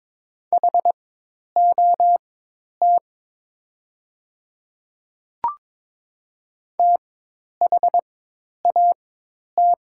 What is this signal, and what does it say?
Morse code, heard as a keyed tone, about 22 wpm, spaced as for 10 wpm, 700 Hz. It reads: HOT THAT
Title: 4x Character Spacing - Top 100 Words - 22wpm